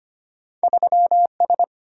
Morse code, heard as a keyed tone, about 34 wpm, 700 Hz. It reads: 3S